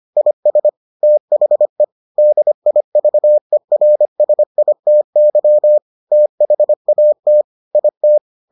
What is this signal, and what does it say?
Morse code, heard as a keyed tone, about 25 wpm, 600 Hz. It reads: IS THE DIVERSITY THAT IT